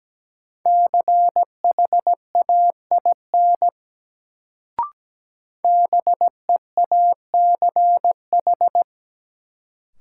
Morse code, heard as a keyed tone, about 17 wpm, 700 Hz. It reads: CHAIN BEACH
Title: Sentences from Top 1500 Words - New Word Drill Alt - 17wpm